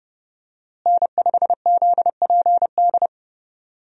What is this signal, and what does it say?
Morse code, heard as a keyed tone, about 30 wpm, 700 Hz. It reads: N5ZPD
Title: USA Call Signs - 5 Characters - Repeated 2x - 30wpm